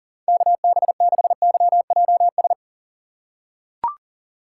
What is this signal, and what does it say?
Morse code, heard as a keyed tone, about 40 wpm, 700 Hz. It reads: KB6YJS
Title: USA Call Signs - 6 Characters - 3x Character Spacing - 40wpm